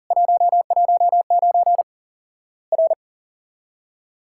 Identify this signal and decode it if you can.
Morse code, heard as a keyed tone, about 40 wpm, 700 Hz. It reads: 119 R
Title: Worked All Europe DX Contest - QTC Exchange - 40wpm